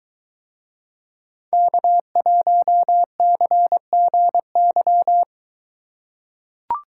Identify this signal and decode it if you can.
Morse code, heard as a keyed tone, about 23 wpm, 700 Hz. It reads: K1CGY